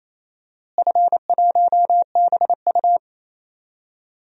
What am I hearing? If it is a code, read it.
Morse code, heard as a keyed tone, about 28 wpm, 700 Hz. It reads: F1BU
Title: DXCC-100 Call Signs - 4 Characters - Repeated 3x - 28wpm